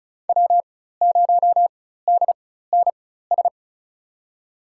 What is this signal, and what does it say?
Morse code, heard as a keyed tone, about 34 wpm, spaced as for 16 wpm, 700 Hz. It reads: W0DNS
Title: USA Call Signs - 5 Characters - 4x Character Spacing - 35wpm